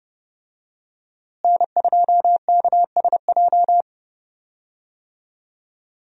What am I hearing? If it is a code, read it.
Morse code, heard as a keyed tone, about 30 wpm, 700 Hz. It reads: N2KSJ
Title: USA Call Signs - 5 Characters - Repeated 3x - 30wpm